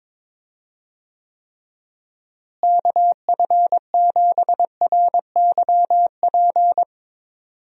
Morse code, heard as a keyed tone, about 22 wpm, 700 Hz. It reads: KF7RYP
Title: USA Call Signs - 6 Characters - Repeated 3x - 22wpm